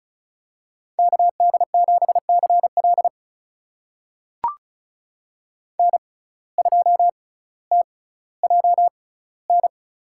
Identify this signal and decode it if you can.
Morse code, heard as a keyed tone, about 36 wpm, spaced as for 18 wpm, 700 Hz. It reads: KD7CL N2TJN
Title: USA Call Signs - 5 Characters - 6x Character Spacing - 35wpm